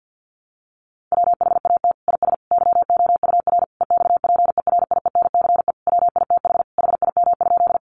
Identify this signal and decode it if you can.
Morse code, heard as a keyed tone, about 50 wpm, 700 Hz. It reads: WHAT IS YOUR EXPERIENCE WITH HIM?
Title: Sentences from Top 600 Words 50wpm